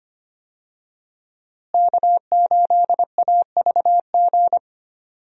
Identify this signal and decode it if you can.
Morse code, heard as a keyed tone, about 25 wpm, 700 Hz. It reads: K8AVG
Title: USA Call Signs - 5 Characters - Repeated 3x - 25wpm